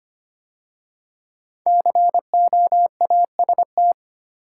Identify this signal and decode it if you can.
Morse code, heard as a keyed tone, about 25 wpm, 700 Hz. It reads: COAST
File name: 5 Letter Words - First 3 Letters Given - 25wpm